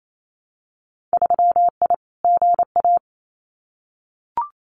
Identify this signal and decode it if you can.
Morse code, heard as a keyed tone, about 28 wpm, 700 Hz. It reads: 3I GA
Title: Winter Field Day Exchanges 28wpm